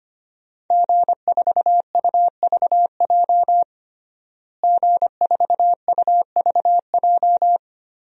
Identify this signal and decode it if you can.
Morse code, heard as a keyed tone, about 25 wpm, 700 Hz. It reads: G4UVJ G4UVJ
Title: DXCC-100 Call Signs - 5 Characters - Repeated 3x - 25wpm